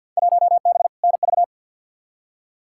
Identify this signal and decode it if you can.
Morse code, heard as a keyed tone, about 47 wpm, 700 Hz. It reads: 1B NV
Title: Field Day Exchanges - 50wpm